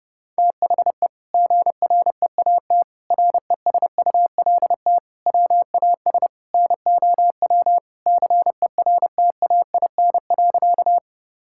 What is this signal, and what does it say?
Morse code, heard as a keyed tone, about 28 wpm, 700 Hz. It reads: THE GREAT RESULT WAS NOW CERTAIN.
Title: Sentences from Top 1000 Words - Review Part 5 - 30wpm